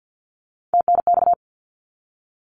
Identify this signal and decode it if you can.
Morse code, heard as a keyed tone, about 50 wpm, 700 Hz. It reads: TNX
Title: Abbreviations 50wpm